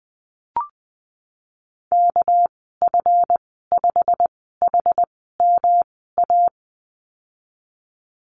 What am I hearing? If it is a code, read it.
Morse code, heard as a keyed tone, about 20 wpm, spaced as for 15 wpm, 700 Hz. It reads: KF5HMA
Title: 219 Lesson 35g - Number 0 - Call Signs 20wpm